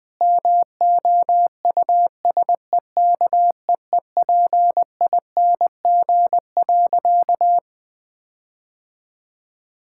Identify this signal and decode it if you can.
Morse code, heard as a keyed tone, about 21 wpm, 700 Hz. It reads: MOUSEKEEPING.
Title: All-Age Appropriate Jokes - 20wpm